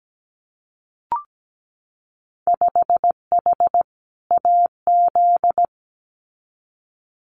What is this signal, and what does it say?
Morse code, heard as a keyed tone, about 17 wpm, 700 Hz. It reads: 5H AZ
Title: Winter Field Day Exchanges Encore 17wpm